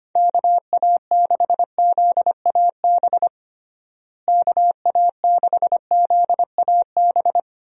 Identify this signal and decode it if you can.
Morse code, heard as a keyed tone, about 25 wpm, 700 Hz. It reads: KA6ZAB KA6ZAB
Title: USA Call Signs - 6 Characters - Repeated 2x - 25wpm